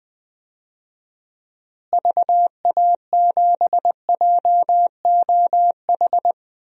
Morse code, heard as a keyed tone, about 20 wpm, 700 Hz. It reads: VA7JOH
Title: DXCC-100 Call Signs - 6 Characters - Repeated 2x - 20wpm